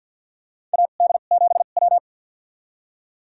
Morse code, heard as a keyed tone, about 46 wpm, spaced as for 34 wpm, 700 Hz. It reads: AD7W
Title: USA Call Signs - 4 Characters - 2x Character Spacing - 50wpm